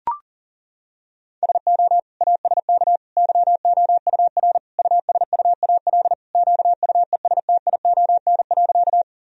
Morse code, heard as a keyed tone, about 40 wpm, 700 Hz. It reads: SO ASK YOUR USUAL QUESTION.
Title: Sentences from Top 1000 Words - Review Part 2 - 40wpm